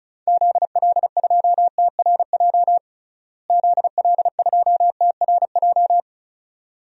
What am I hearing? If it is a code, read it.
Morse code, heard as a keyed tone, about 36 wpm, 700 Hz. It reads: ZL2TRJ ZL2TRJ
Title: DXCC-100 Call Signs - 6 Characters - Repeated 2x - 35wpm